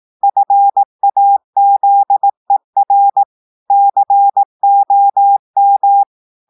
Morse code, heard as a keyed tone, about 18 wpm, 800 Hz. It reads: FAZER COM
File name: audios / portugais / portugais texte en clair 2 vitesse 18 mots minutes